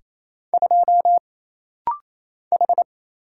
Morse code, heard as a keyed tone, about 28 wpm, 700 Hz. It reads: 2 H
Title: Single Letter-Number - Rapid-Fire 28wpm